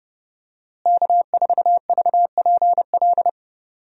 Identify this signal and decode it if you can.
Morse code, heard as a keyed tone, about 30 wpm, 700 Hz. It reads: K4VPL
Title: USA Call Signs - 5 Characters - Repeated 3x - 30wpm